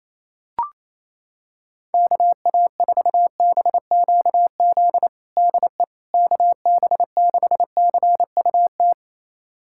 Morse code, heard as a keyed tone, about 28 wpm, 700 Hz. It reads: KA4BQZ DE KB6CUT
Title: USA Callsign DE Callsign - 6 Characters - 28wpm